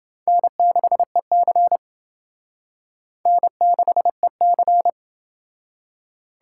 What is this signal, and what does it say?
Morse code, heard as a keyed tone, about 28 wpm, 700 Hz. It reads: N6EC N6EC